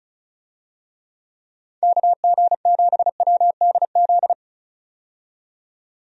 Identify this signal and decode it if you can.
Morse code, heard as a keyed tone, about 36 wpm, 700 Hz. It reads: KG7WDZ
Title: USA Call Signs - 6 Characters - 35wpm